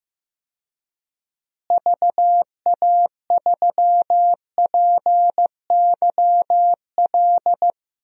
Morse code, heard as a keyed tone, about 15 wpm, 700 Hz. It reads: VA3PYL
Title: DXCC-100 Call Signs - 6 Characters - Repeated 3x - 15wpm